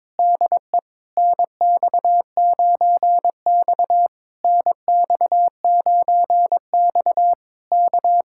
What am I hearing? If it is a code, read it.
Morse code, heard as a keyed tone, about 21 wpm, 700 Hz. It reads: DE NX9X NX9X K
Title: Speed-Racing Common QSO Phrases - 22wpm